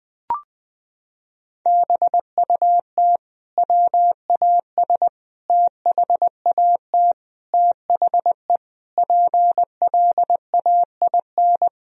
Morse code, heard as a keyed tone, about 20 wpm, 700 Hz. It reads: BUT WAS THAT THE PLAIN
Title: Sentences from Top 1000 Words - Review Part 2 - 20wpm